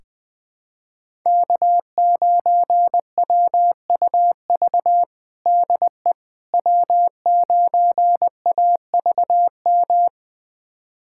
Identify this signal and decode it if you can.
Morse code, heard as a keyed tone, about 21 wpm, 700 Hz. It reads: K9WUV DE W9AVM
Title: USA Callsign DE Callsign - 5 Characters - 20wpm